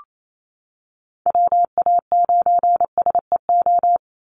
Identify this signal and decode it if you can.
Morse code, heard as a keyed tone, about 28 wpm, 700 Hz. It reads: WA9SEO
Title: Call Signs Repeated 3x - 28wpm